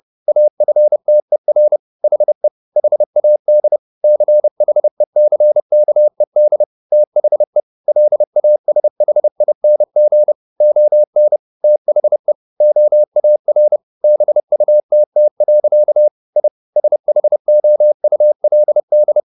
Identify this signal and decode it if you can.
Morse code, heard as a keyed tone, about 30 wpm, 600 Hz. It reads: AFTER HE HAD CHECKED THE LASHING ON THE OAR BUTT. I SHOULD